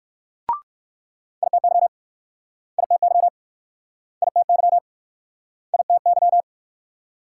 Morse code, heard as a keyed tone, about 62 wpm, 690 Hz. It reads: ITY ITY ITY ITY